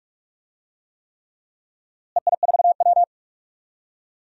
Morse code, heard as a keyed tone, about 46 wpm, 700 Hz. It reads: EI4W
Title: DXCC-100 Call Signs - 4 Characters - Repeated 3x - 45wpm